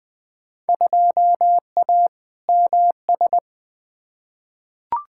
Morse code, heard as a keyed tone, about 21 wpm, 700 Hz. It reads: 2A MS
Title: Field Day Exchanges - 20wpm